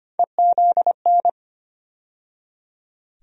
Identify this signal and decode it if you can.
Morse code, heard as a keyed tone, about 25 wpm, 700 Hz. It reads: EZN
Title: USA Call Sign Postfixes - 3 Letters 25wpm